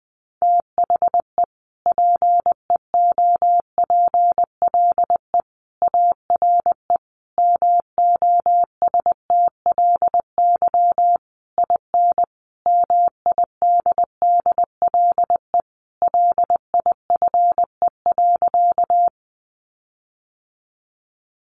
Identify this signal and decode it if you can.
Morse code, heard as a keyed tone, about 20 wpm, 700 Hz. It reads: THE PEOPLE ARE MOSTLY IN MIDDLE LIFE.
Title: Sentences from Top 1300 Words 20wpm